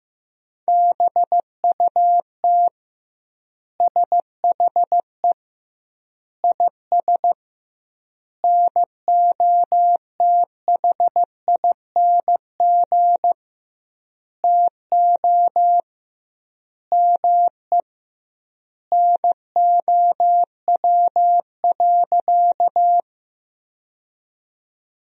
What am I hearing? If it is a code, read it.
Morse code, heard as a keyed tone, about 16 wpm, 700 Hz. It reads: BUT SHE IS NOTHING TO ME NOW.
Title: 2x Word-Spacing - Sentences from Top 500 Words - 15wpm